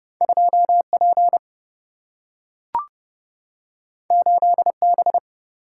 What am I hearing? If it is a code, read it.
Morse code, heard as a keyed tone, about 29 wpm, 700 Hz. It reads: 2P 8B
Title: Single-Letter Single-Number 30wpm